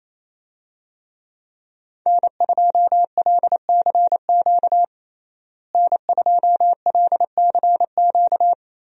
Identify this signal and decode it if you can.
Morse code, heard as a keyed tone, about 28 wpm, 700 Hz. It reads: N2LCQ N2LCQ
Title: USA Call Signs - 5 Characters - Repeated 2x - 28wpm